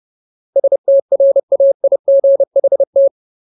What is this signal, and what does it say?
Morse code, heard as a keyed tone, about 30 wpm, 550 Hz. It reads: STRAIGHT
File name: Ultimate Word List / 30WPM Ultimate